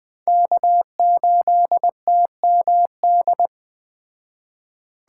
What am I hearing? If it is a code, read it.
Morse code, heard as a keyed tone, about 20 wpm, 700 Hz. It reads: K8TMD
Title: Speed Racing - USA 1x3 Call Signs 17wpm